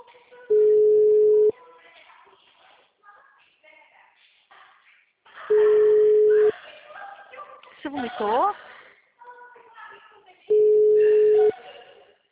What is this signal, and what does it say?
Ringback tone (Europe, 425 Hz), with speech.
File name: out- -2011- - - .5771